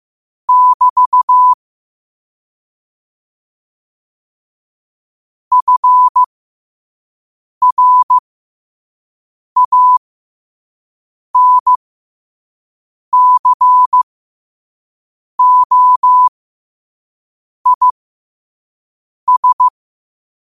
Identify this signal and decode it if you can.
Morse code, heard as a keyed tone, about 15 wpm, spaced as for 5 wpm, 1000 Hz. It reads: <BT> FRANCOIS
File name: qotdfiles / 5wpm